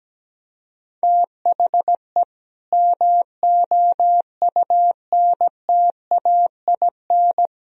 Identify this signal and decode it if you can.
Morse code, heard as a keyed tone, about 17 wpm, 700 Hz. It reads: THE MOUNTAIN
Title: Sentences from Top 1000 Words - Review Part 1 - 17wpm